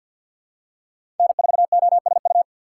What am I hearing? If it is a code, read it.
Morse code, heard as a keyed tone, about 46 wpm, 700 Hz. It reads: N4OSU
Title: USA Call Signs - 5 Characters - 3x Character Spacing - 50wpm